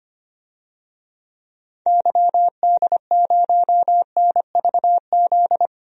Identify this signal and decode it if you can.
Morse code, heard as a keyed tone, about 25 wpm, 700 Hz. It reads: YD0NVZ
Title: DXCC-100 Call Signs - 6 Characters - Repeated 3x - 25wpm